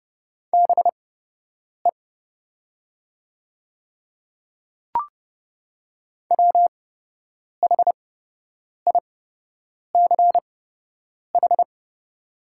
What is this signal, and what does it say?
Morse code, heard as a keyed tone, about 30 wpm, spaced as for 8 wpm, 700 Hz. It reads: BE WHICH
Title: 8x Character Spacing - Top 100 Words - 30wpm